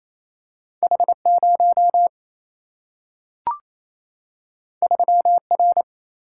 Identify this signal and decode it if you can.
Morse code, heard as a keyed tone, about 28 wpm, 700 Hz. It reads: H0 3R
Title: Single-Letter Single-Number 28wpm